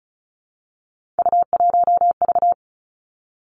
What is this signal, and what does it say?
Morse code, heard as a keyed tone, about 36 wpm, 700 Hz. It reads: U1V
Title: Three Challenging Letter-Number Combinations 35wpm